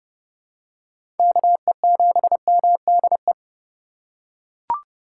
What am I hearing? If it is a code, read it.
Morse code, heard as a keyed tone, about 30 wpm, 700 Hz. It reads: KE7MDE
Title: USA Call Signs - 6 Characters - 4x Character Spacing - 30wpm